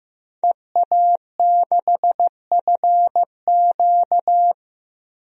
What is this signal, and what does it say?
Morse code, heard as a keyed tone, about 15 wpm, 700 Hz. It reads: EA6FQ
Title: DXCC-100 Call Signs - 5 Characters - 15wpm